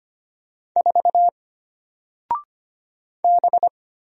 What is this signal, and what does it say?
Morse code, heard as a keyed tone, about 25 wpm, 700 Hz. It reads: 4 B
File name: Single-Letter-Number-25wpm